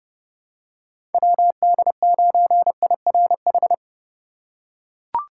Morse code, heard as a keyed tone, about 32 wpm, 700 Hz. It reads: WD9IRH
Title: USA Call Signs - 6 Characters - 3x Character Spacing - 30wpm